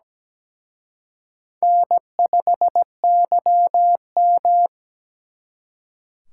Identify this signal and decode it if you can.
Morse code, heard as a keyed tone, about 21 wpm, 700 Hz. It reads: N5YM
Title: Speed Racing - USA 1x2 Call Signs 17wpm